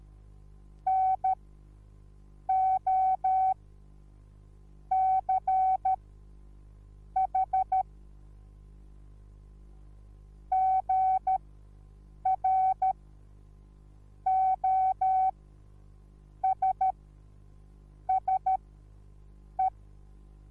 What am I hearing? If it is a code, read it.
Morse code, heard as a keyed tone, about 13 wpm, spaced as for 6 wpm, 760 Hz. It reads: NOCH GROSSE
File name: Onlinelehrgang / morsen / DJ4UF Morsen21